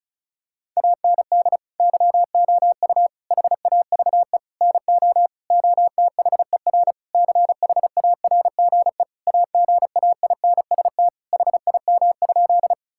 Morse code, heard as a keyed tone, about 36 wpm, 700 Hz. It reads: AND YOU HAVE NO OTHER CHARGE AGAINST HIM?
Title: Sentences from Top 1000 Words - Review Part 6 - 35wpm